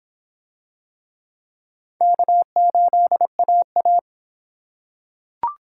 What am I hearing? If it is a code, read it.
Morse code, heard as a keyed tone, about 25 wpm, 700 Hz. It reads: K8AA